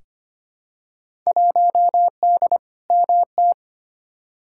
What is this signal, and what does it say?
Morse code, heard as a keyed tone, about 25 wpm, 700 Hz. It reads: 1D MT
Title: Field Day Exchanges - 25wpm